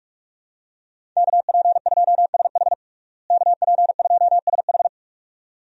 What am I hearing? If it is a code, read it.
Morse code, heard as a keyed tone, about 46 wpm, 700 Hz. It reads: KP2SH KP2SH
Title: DXCC-100 Call Signs - 5 Characters - Repeated 2x - 45wpm